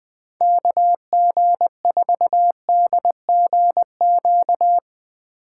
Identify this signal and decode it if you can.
Morse code, heard as a keyed tone, about 22 wpm, 700 Hz. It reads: KG4DGQ
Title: Speed Racing - USA Call Signs 20wpm